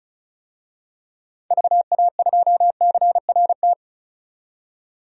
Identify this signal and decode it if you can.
Morse code, heard as a keyed tone, about 34 wpm, 700 Hz. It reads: VA2CRT
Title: DXCC-100 Call Signs - 6 Characters - 35wpm